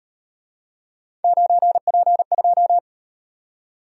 Morse code, heard as a keyed tone, about 33 wpm, 700 Hz. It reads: 9P2